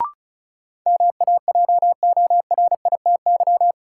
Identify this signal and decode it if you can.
Morse code, heard as a keyed tone, about 36 wpm, 700 Hz. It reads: MAJORITY
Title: Sets of 3 Words - New Word Drill - 35wpm